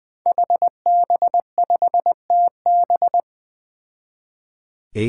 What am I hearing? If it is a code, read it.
Morse code, heard as a keyed tone, about 20 wpm, 700 Hz. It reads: HB5TB
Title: Five Challenging Letter-Number Combinations 20wpm